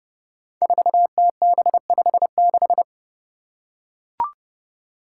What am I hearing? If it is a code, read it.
Morse code, heard as a keyed tone, about 30 wpm, 700 Hz. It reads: 4TB56
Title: Five Challenging Letter-Number Combinations 30wpm